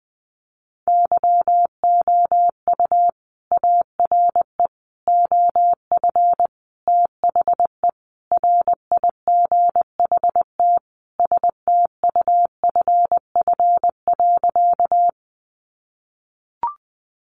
Morse code, heard as a keyed tone, about 20 wpm, 700 Hz. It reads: YOU ARE OF THE RIGHT STUFF.
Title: Sentences from Top 1400 Words 20wpm